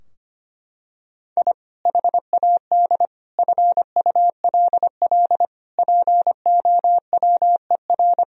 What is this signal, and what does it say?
Morse code, heard as a keyed tone, about 25 wpm, 700 Hz. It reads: I HAD FULL POWER
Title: Sentences from Top 1000 Words - Review Part 3 - 25wpm